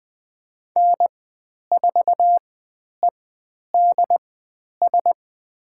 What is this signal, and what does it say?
Morse code, heard as a keyed tone, about 20 wpm, spaced as for 10 wpm, 700 Hz. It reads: N4EDS
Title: 102 Lesson 17g - Letter u - Call Signs 20wpm